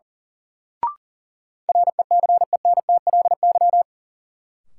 Morse code, heard as a keyed tone, about 37 wpm, 700 Hz. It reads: RECENTLY
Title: Sentences from Top 1200 Words - New Word Drill Alt - 40wpm